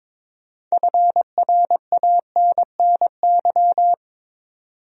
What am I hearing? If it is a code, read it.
Morse code, heard as a keyed tone, about 22 wpm, 700 Hz. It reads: FRANNY
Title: USA Common Names - 22wpm